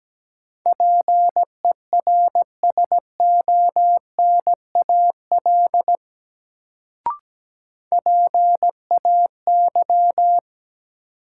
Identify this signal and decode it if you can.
Morse code, heard as a keyed tone, about 17 wpm, 700 Hz. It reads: PERSONAL PAY
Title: Sentences from Top 700 Words - New Word Drill - 17wpm